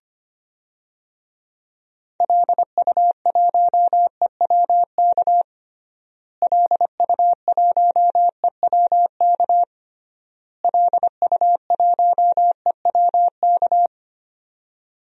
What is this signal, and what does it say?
Morse code, heard as a keyed tone, about 25 wpm, 700 Hz. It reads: LU1EWK LU1EWK LU1EWK
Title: DXCC-100 Call Signs - 6 Characters - Repeated 3x - 25wpm